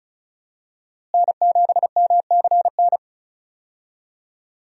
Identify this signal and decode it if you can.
Morse code, heard as a keyed tone, about 36 wpm, 700 Hz. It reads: N7MCN
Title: USA Call Signs - 5 Characters - Repeated 2x - 35wpm